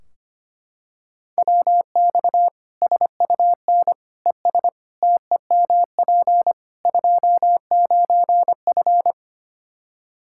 Morse code, heard as a keyed tone, about 25 wpm, 700 Hz. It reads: WX SUN ES TEMP 29F
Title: K1USN SST QSO Practice 25wpm